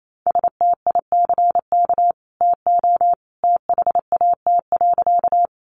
Morse code, heard as a keyed tone, about 28 wpm, 700 Hz. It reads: STICK TO THAT.
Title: Sentences from Top 1200 Words 28wpm